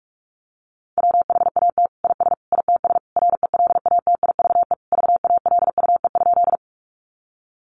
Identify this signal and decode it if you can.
Morse code, heard as a keyed tone, about 44 wpm, 700 Hz. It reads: WHAT IS ITS RELATIVE VALUE?
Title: Sentences from Top 1400 Words 45wpm